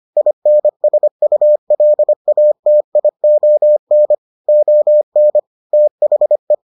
Morse code, heard as a keyed tone, about 25 wpm, 600 Hz. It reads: INSULATION ON THE